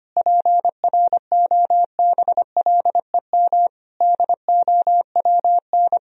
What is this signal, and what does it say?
Morse code, heard as a keyed tone, about 25 wpm, 700 Hz. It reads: PROBLEM DOWN
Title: Sentences from Top 1000 Words - Review Part 4 - 25wpm